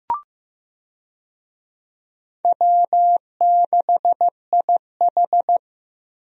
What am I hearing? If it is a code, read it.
Morse code, heard as a keyed tone, about 15 wpm, 700 Hz. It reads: W6IH
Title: USA Call Signs - 4 Characters - 15wpm